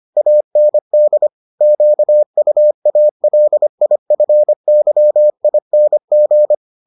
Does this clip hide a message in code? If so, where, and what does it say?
Yes: Morse code from 0.2 to 6.5 s, heard as a keyed tone, about 25 wpm, 600 Hz. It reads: AND QUALIFYING